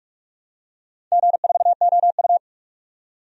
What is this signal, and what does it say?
Morse code, heard as a keyed tone, about 43 wpm, 700 Hz. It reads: G4OU
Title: DXCC-100 Call Signs - 4 Characters - 45wpm